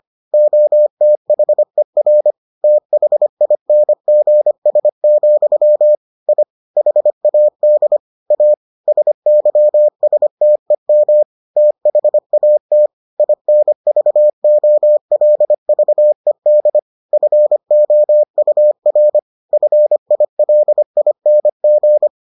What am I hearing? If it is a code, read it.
Morse code, heard as a keyed tone, about 25 wpm, 600 Hz. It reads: OTHER THINGS, I HAD A SYSTEM THAT INVOLVED FOUR FILING